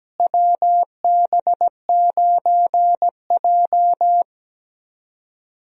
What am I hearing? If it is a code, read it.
Morse code, heard as a keyed tone, about 17 wpm, 700 Hz. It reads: WB9J
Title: USA Call Signs - 4 Characters - Repeated 3x - 17wpm